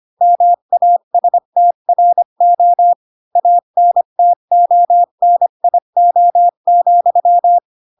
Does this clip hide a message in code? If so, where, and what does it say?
Yes: Morse code from 0.2 to 7.6 s, heard as a keyed tone, about 25 wpm, 700 Hz. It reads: MASTRO ANTONIO,